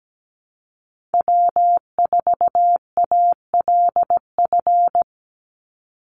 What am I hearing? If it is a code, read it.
Morse code, heard as a keyed tone, about 17 wpm, 700 Hz. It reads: W4ALF
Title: Call Signs Repeated 2x - 17wpm